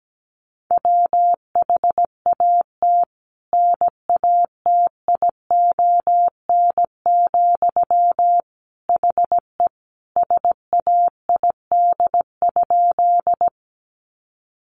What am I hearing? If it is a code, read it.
Morse code, heard as a keyed tone, about 17 wpm, 700 Hz. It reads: WHAT NATION, HE SAID?
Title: Sentences from Top 900 Words 17wpm